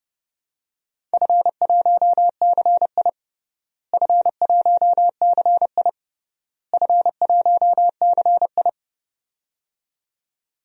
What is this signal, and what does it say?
Morse code, heard as a keyed tone, about 30 wpm, 700 Hz. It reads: F1CI F1CI F1CI
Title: DXCC-100 Call Signs - 4 Characters - Repeated 3x - 30wpm